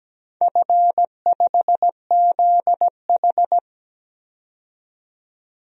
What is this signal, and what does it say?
Morse code, heard as a keyed tone, about 17 wpm, 700 Hz. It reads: F5ZH
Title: DXCC-100 Call Signs - 4 Characters - 17wpm